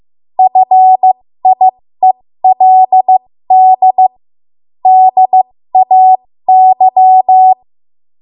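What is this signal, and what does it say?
Morse code, heard as a keyed tone, about 15 wpm, spaced as for 13 wpm, 750 Hz. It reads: FIELD DAY
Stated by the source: 13 wpm Morse code practice archive transmission.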